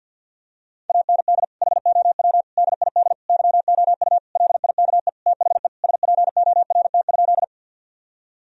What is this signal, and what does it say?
Morse code, heard as a keyed tone, about 50 wpm, 700 Hz. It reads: AND HOW DID YOU LIKE THE SPORT?
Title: Sentences from Top 1000 Words - Review Part 4 - 50wpm